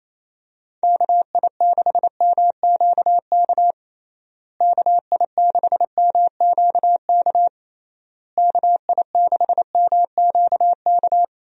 Morse code, heard as a keyed tone, about 28 wpm, 700 Hz. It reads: KI6MQK KI6MQK KI6MQK
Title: USA Call Signs - 6 Characters - Repeated 3x - 28wpm